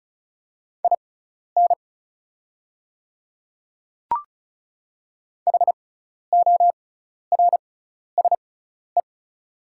Morse code, heard as a keyed tone, about 36 wpm, spaced as for 12 wpm, 690 Hz. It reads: IN HORSE
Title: 6x Character Spacing - Top 300 Words - 35wpm